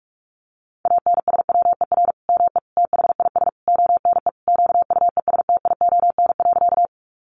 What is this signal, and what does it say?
Morse code, heard as a keyed tone, about 44 wpm, 700 Hz. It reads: ANSWER ME THIS ONE QUESTION.
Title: Sentences from Top 400 Words 45wpm